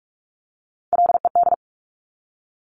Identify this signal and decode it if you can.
Morse code, heard as a keyed tone, about 46 wpm, 700 Hz. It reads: LED
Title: Top Three Letter Combinations in Words 45wpm